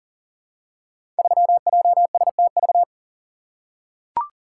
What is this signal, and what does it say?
Morse code, heard as a keyed tone, about 40 wpm, 700 Hz. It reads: 3JSTV
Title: Code Groups - 5 Letter:Numbers - 40wpm